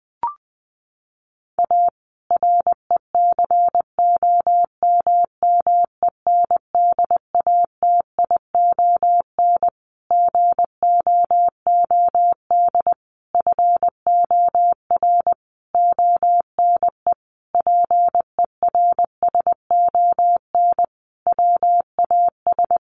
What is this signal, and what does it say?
Morse code, heard as a keyed tone, about 19 wpm, 700 Hz. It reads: A RECOMMENDATION GOOD FOR ONE PERSON WAS
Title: Sentences from Top 1400 Words 20wpm